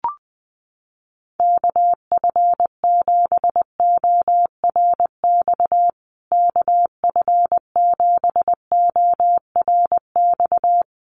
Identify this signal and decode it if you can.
Morse code, heard as a keyed tone, about 20 wpm, 700 Hz. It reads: KF7ORX KF7ORX
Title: USA Call Signs Repeated 3x 20wpm